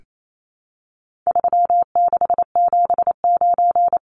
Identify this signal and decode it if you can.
Morse code, heard as a keyed tone, about 28 wpm, 700 Hz. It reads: 3679
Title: Numbers - 4-Digit 28wpm